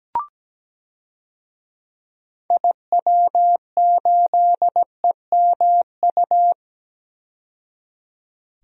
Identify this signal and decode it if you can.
Morse code, heard as a keyed tone, about 17 wpm, 700 Hz. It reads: IW8EMU
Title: DXCC-100 Call Signs - 6 Characters - 17wpm